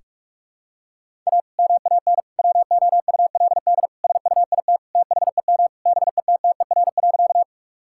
Morse code, heard as a keyed tone, about 46 wpm, 700 Hz. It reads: A MAN WOULD SUIT THEM BETTER.
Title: Sentences from Top 1600 Words 45wpm